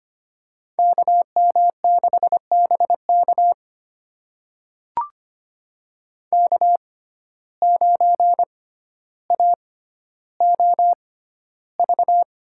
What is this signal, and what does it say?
Morse code, heard as a keyed tone, about 25 wpm, spaced as for 13 wpm, 700 Hz. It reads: KM6BK K9AOV
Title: USA Call Signs - 5 Characters - 6x Character Spacing - 25wpm